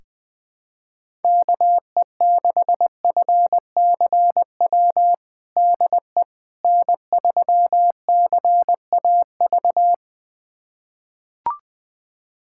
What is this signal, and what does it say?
Morse code, heard as a keyed tone, about 20 wpm, 700 Hz. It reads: KE6FCW DE N3CAV
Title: USA Callsign DE Callsign 20wpm